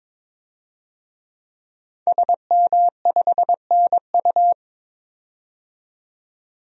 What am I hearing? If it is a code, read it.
Morse code, heard as a keyed tone, about 22 wpm, 700 Hz. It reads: SM5NU
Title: DXCC-100 Call Signs - 5 Characters - 22wpm